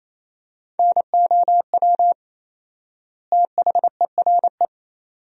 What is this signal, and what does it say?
Morse code, heard as a keyed tone, about 28 wpm, 700 Hz. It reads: NOW THERE
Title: 4x Word-Spacing - Sentences from Top 100 Words - 28wpm